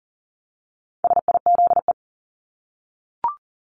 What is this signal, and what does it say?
Morse code, heard as a keyed tone, about 40 wpm, 700 Hz. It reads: SIZE
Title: Top 500 Words Repeated 40wpm